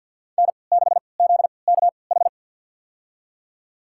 Morse code, heard as a keyed tone, about 50 wpm, spaced as for 28 wpm, 700 Hz. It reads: N6ZKH
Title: USA Call Signs - 5 Characters - 3x Character Spacing - 50wpm